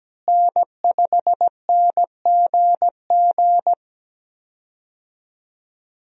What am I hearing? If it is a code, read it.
Morse code, heard as a keyed tone, about 17 wpm, 700 Hz. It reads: N5NGG
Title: USA Call Signs - 5 Characters - Repeated 3x - 17wpm